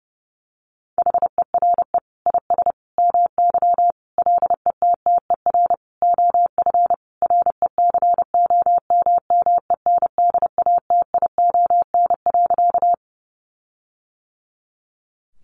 Morse code, heard as a keyed tone, about 30 wpm, 700 Hz. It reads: HERE IS MY LETTER OF RECOMMENDATION.
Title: Sentences from Top 1400 Words 30wpm